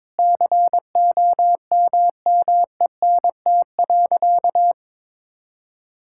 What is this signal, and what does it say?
Morse code, heard as a keyed tone, about 22 wpm, 700 Hz. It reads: COMMENT.
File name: practice / 22wpm / Sentences with 8-Word Phrases - 22wpm